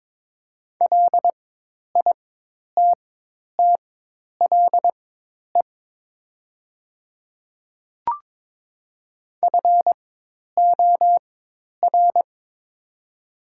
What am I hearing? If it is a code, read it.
Morse code, heard as a keyed tone, about 22 wpm, spaced as for 10 wpm, 700 Hz. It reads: LITTLE FOR
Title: 4x Character Spacing - Top 200 Words - 22wpm